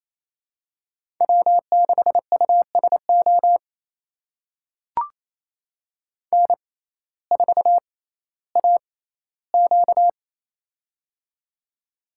Morse code, heard as a keyed tone, about 28 wpm, spaced as for 15 wpm, 700 Hz. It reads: W6USO N4AQ
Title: USA Call Signs - 6x Character Spacing - 28wpm